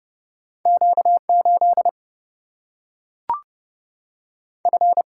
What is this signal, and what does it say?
Morse code, heard as a keyed tone, about 30 wpm, 700 Hz. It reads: Q 8 F
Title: Single-Letter Single-Number 30wpm